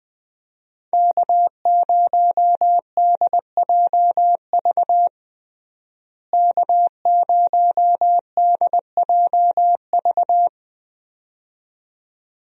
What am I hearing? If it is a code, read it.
Morse code, heard as a keyed tone, about 21 wpm, 700 Hz. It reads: K0DJV K0DJV
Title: USA Call Signs - 5 Characters - Repeated 2x - 20wpm